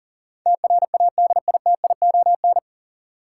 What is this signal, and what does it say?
Morse code, heard as a keyed tone, about 40 wpm, 700 Hz. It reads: TRADITION
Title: Sentences from Top 1500 Words - New Word Drill Alt - 40wpm